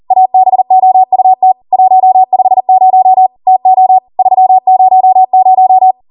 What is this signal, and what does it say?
Morse code, heard as a keyed tone, about 40 wpm, 750 Hz. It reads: ABOUT 150 TO 300